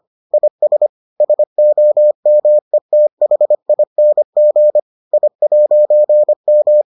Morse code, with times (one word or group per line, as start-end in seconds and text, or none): 0.3-0.9 IS
1.2-4.8 SOMETHING
5.1-6.8 I'M